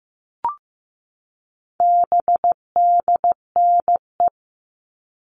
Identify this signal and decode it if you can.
Morse code, heard as a keyed tone, about 15 wpm, 700 Hz. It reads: BDNE
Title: All Cut Numbers - 4-Digit 15wpm